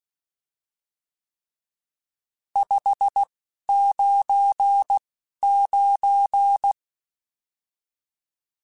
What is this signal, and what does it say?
Morse code, heard as a keyed tone, about 16 wpm, spaced as for 11 wpm, 800 Hz. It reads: 599
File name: RSTs / RST-mp3s / Various RST @10